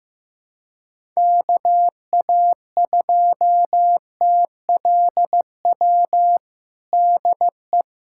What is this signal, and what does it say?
Morse code, heard as a keyed tone, about 16 wpm, 700 Hz. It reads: KA2TLW DE
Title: USA Callsign DE Callsign - 6 Characters - 15wpm